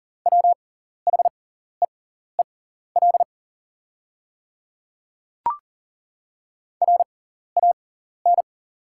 Morse code, heard as a keyed tone, about 40 wpm, spaced as for 14 wpm, 690 Hz. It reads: WHEEL RAN
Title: 6x Character Spacing - Top 500 Words - 40wpm